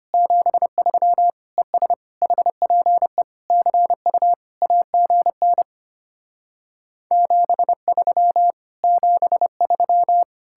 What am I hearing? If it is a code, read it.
Morse code, heard as a keyed tone, about 33 wpm, 700 Hz. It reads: 73 ES HPE CU AGN 73 73